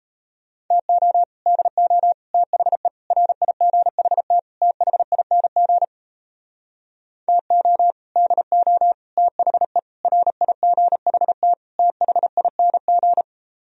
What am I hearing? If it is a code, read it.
Morse code, heard as a keyed tone, about 37 wpm, 700 Hz. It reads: TO DO THE RIGHT THING TO DO THE RIGHT THING